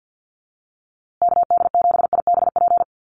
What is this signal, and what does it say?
Morse code, heard as a keyed tone, about 50 wpm, 700 Hz. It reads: KD7IBP
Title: USA Call Signs Repeated 3x 50wpm